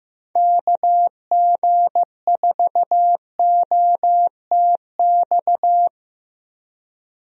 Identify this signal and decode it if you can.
Morse code, heard as a keyed tone, about 16 wpm, 700 Hz. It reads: KG4OTX
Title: USA Call Signs - 6 Characters - Repeated 3x - 15wpm